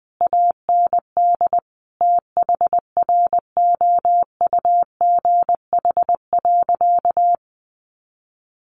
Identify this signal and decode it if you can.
Morse code, heard as a keyed tone, about 20 wpm, 700 Hz. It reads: AND THROUGH.
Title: Sentences from Top 1400 Words 20wpm